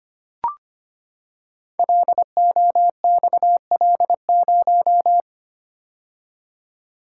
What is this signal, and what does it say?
Morse code, heard as a keyed tone, about 25 wpm, 700 Hz. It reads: LOXL0
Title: Code Groups - 5 Letter:Numbers - 25wpm